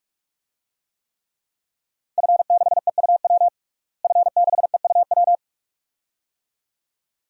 Morse code, heard as a keyed tone, about 46 wpm, 700 Hz. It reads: F6EUW F6EUW
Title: DXCC-100 Call Signs - 5 Characters - Repeated 2x - 45wpm